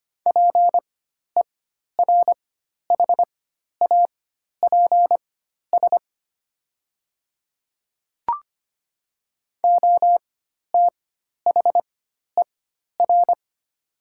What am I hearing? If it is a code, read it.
Morse code, heard as a keyed tone, about 25 wpm, spaced as for 12 wpm, 700 Hz. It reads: PERHAPS OTHER
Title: 4x Character Spacing - Top 500 Words - 25wpm